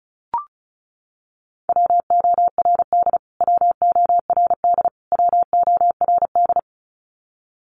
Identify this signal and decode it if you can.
Morse code, heard as a keyed tone, about 34 wpm, 700 Hz. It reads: WORD WORD WORD
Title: Top 200 Words Repeated 35wpm